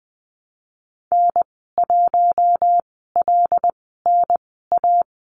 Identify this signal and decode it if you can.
Morse code, heard as a keyed tone, about 19 wpm, spaced as for 14 wpm, 700 Hz. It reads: N1LNA
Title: 076 Lesson 13g - Letter L - Call Signs 20wpm